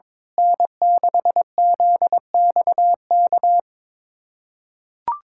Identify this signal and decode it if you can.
Morse code, heard as a keyed tone, about 22 wpm, 700 Hz. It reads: N6ZXK
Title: USA Callsign DE Callsign - 5 Characters - 22wpm